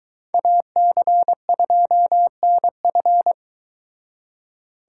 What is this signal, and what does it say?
Morse code, heard as a keyed tone, about 22 wpm, 700 Hz. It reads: AC2NF